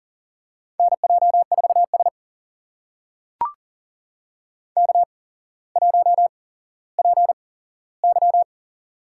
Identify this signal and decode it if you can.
Morse code, heard as a keyed tone, about 37 wpm, spaced as for 17 wpm, 700 Hz. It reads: NJ4S K1PY
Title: USA Call Signs - 4 Characters - 8x Character Spacing - 40wpm